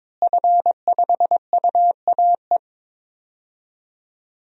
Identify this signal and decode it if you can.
Morse code, heard as a keyed tone, about 22 wpm, 700 Hz. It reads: F5UAE
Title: DXCC-100 Call Signs - 5 Characters - Repeated 3x - 22wpm